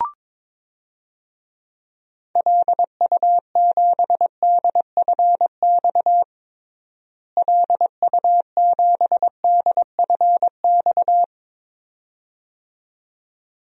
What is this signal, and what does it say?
Morse code, heard as a keyed tone, about 22 wpm, 700 Hz. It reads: LU7DFX LU7DFX
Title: DXCC-100 Call Signs - 6 Characters - Repeated 2x - 22wpm